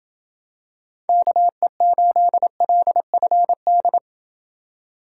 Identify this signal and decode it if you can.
Morse code, heard as a keyed tone, about 28 wpm, 700 Hz. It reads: KE8LFD